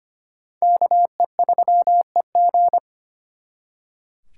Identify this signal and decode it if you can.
Morse code, heard as a keyed tone, about 32 wpm, 700 Hz. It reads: KE3EG